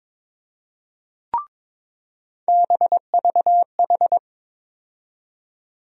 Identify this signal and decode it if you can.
Morse code, heard as a keyed tone, about 22 wpm, 700 Hz. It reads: BVH
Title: USA Call Sign Postfixes - 3 Letters 22wpm